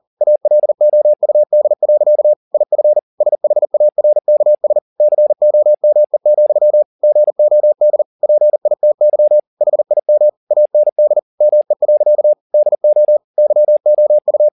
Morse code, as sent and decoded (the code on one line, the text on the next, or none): .- .-.. --- ..- -.. .-.-.- / .. ..-. / ... .... .- .-. -.- ... / -.-. --- -- . --..-- / --. --- -.. / .--. .. - -.-- / .... .. -- / .- -. -.. / -- . .-.-.- / -.. --- / -.-- --- ..-
ALOUD. IF SHARKS COME, GOD PITY HIM AND ME. DO YOU